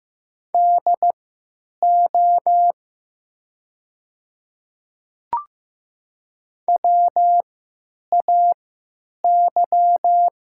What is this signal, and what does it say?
Morse code, heard as a keyed tone, about 15 wpm, spaced as for 8 wpm, 700 Hz. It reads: DO WAY
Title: 3x Character Spacing - Top 200 Words - 15wpm